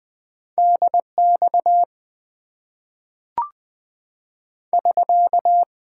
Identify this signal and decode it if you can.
Morse code, heard as a keyed tone, about 20 wpm, 700 Hz. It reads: DX <SK>
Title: Top 90 QSO Elements - 6x Character Spacing - 20wpm